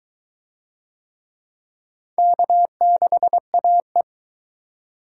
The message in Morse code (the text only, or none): K6AE